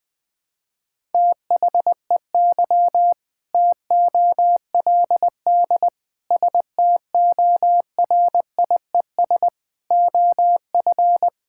Morse code, heard as a keyed tone, about 21 wpm, 700 Hz. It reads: THEY TOLD STORIES OF
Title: Sentences with 2-Word Phrases - 20wpm